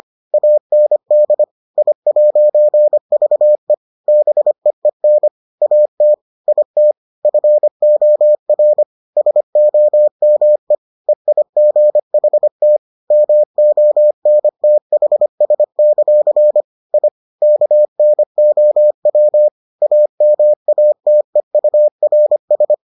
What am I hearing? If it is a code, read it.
Morse code, heard as a keyed tone, about 25 wpm, 600 Hz. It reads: AND I'VE BEEN AT IT FOR SOME EIGHT MONTHS; I KNOW AMATEURS